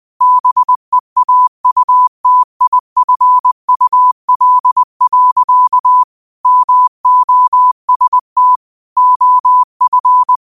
Morse code, heard as a keyed tone, about 20 wpm, 1000 Hz. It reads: BEAUTIFUL. MOST OF
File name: qotdfiles / 20wpm